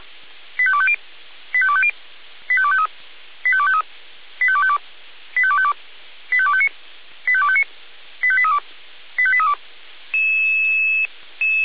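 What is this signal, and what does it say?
Five-tone selective call (ZVEI), heard as quick runs of five tones, tones of about 70 ms. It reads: 85250 85250 85253 85253 85252 85252 85259 85259 85832 85832
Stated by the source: ZVEI five-tone call received by SDR.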